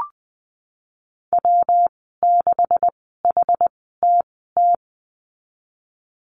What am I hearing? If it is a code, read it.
Morse code, heard as a keyed tone, about 20 wpm, spaced as for 15 wpm, 700 Hz. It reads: W6HTT
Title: 140 Lesson 23g - Letter f - Call Signs 20wpm